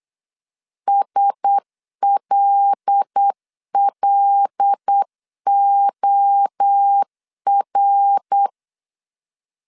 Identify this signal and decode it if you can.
Morse code, heard as a keyed tone, about 8 wpm, 800 Hz. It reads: SLLOR
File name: 8wpm